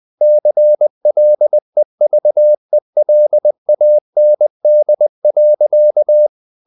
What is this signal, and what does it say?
Morse code, heard as a keyed tone, about 20 wpm, 600 Hz. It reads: CLEVELAND.
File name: the old man and the sea / the old man and the sea part 1 20wpm